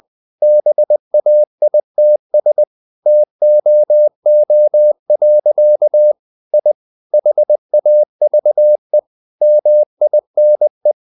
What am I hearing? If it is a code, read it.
Morse code, heard as a keyed tone, about 20 wpm, 600 Hz. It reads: BAITS TOO. I HAVE MINE
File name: the old man and the sea / the old man and the sea part 1 20wpm